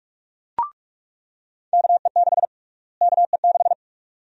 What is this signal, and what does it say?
Morse code, heard as a keyed tone, about 42 wpm, 700 Hz. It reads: KE6 KE6
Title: USA Call Sign Prefixes - Repeated 2x - 45wpm